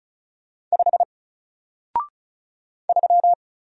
Morse code, heard as a keyed tone, about 36 wpm, 700 Hz. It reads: 5 3
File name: Single-Letter-Number-35wpm